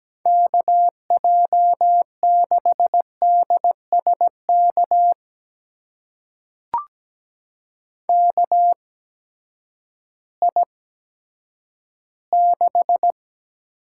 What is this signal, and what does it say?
Morse code, heard as a keyed tone, about 17 wpm, spaced as for 10 wpm, 700 Hz. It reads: KJ6DSK KI6
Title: USA Call Signs - 6 Characters - 8x Character Spacing - 17wpm